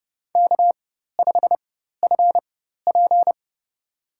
Morse code, heard as a keyed tone, about 29 wpm, spaced as for 14 wpm, 700 Hz. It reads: K5FP
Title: USA Call Signs - 4 Characters - 4x Character Spacing - 30wpm